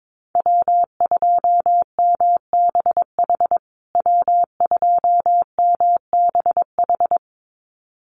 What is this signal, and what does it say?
Morse code, heard as a keyed tone, about 22 wpm, 700 Hz. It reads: W2MBH W2MBH
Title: USA Call Signs Repeated 2x 22wpm